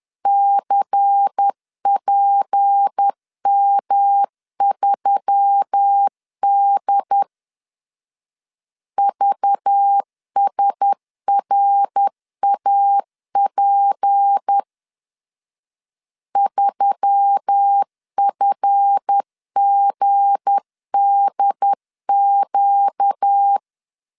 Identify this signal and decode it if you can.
Morse code, heard as a keyed tone, about 10 wpm, 800 Hz. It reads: CPM3D VSRAP 3FGDQ